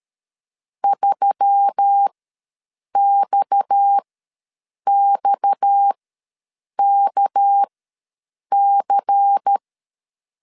Morse code, heard as a keyed tone, about 13 wpm, spaced as for 7 wpm, 800 Hz. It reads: 3XXKC
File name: LEZIONI / 6 PKX0 / 6 PKX ws 12wpm 3106